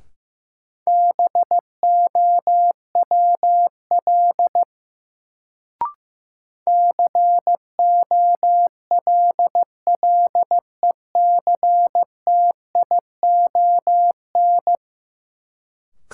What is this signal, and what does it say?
Morse code, heard as a keyed tone, about 15 wpm, 700 Hz. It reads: BOWL COLLECTION
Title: Sentences from Top 1200 Words - New Word Drill - 15wpm